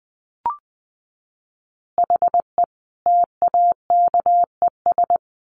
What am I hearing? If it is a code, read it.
Morse code, heard as a keyed tone, about 20 wpm, 700 Hz. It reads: HE TAKES
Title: Additional Easy Phrases 20wpm